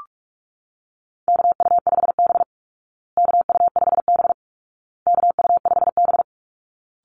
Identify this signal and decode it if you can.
Morse code, heard as a keyed tone, about 44 wpm, 700 Hz. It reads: KU5B KU5B KU5B
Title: Call Signs Repeated 2x - 45wpm